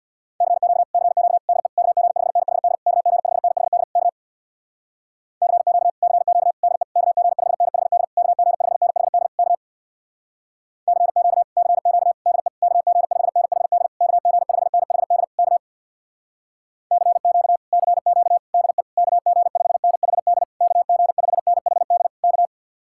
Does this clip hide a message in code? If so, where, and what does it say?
Yes: Morse code from 0.4 to 22.5 s, heard as a keyed tone, about 66 wpm, 700 Hz. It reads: CQ CQ DE KG5NHD KG5NHD K CQ CQ DE KG5NHD KG5NHD K CQ CQ DE KG5NHD KG5NHD K CQ CQ DE KG5NHD KG5NHD K